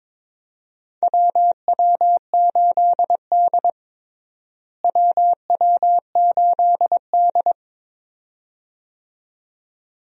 Morse code, heard as a keyed tone, about 22 wpm, 700 Hz. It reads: WW8D WW8D
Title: USA Call Signs - 4 Characters - Repeated 2x - 22wpm